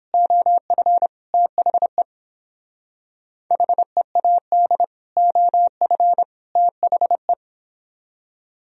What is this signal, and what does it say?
Morse code, heard as a keyed tone, about 31 wpm, 700 Hz. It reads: OF THE HEAD OF THE